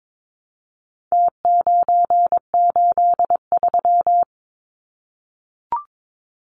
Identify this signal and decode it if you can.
Morse code, heard as a keyed tone, about 22 wpm, 700 Hz. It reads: T983
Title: Numbers - Contest Style - 4-Digit 22wpm